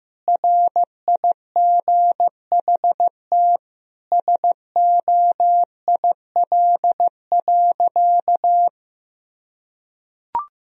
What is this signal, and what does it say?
Morse code, heard as a keyed tone, about 15 wpm, 700 Hz. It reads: RIGHT SOIL.
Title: Sentences from Top 1000 Words - Review Part 4 - 15wpm